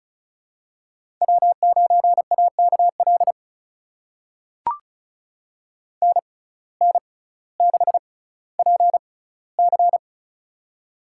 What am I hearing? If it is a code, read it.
Morse code, heard as a keyed tone, about 36 wpm, spaced as for 18 wpm, 700 Hz. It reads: W9AKL NN6PC
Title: USA Call Signs - 5 Characters - 6x Character Spacing - 35wpm